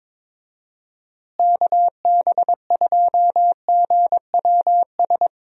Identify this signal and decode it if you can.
Morse code, heard as a keyed tone, about 22 wpm, 700 Hz. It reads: KB2GWS
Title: USA Call Signs - 6 Characters - Repeated 2x - 22wpm